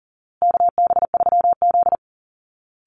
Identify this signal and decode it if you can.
Morse code, heard as a keyed tone, about 40 wpm, 700 Hz. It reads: KB3Z
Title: Call Signs Repeated 2x - 40wpm